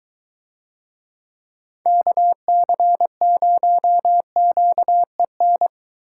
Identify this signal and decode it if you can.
Morse code, heard as a keyed tone, about 22 wpm, 700 Hz. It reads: KC0QEN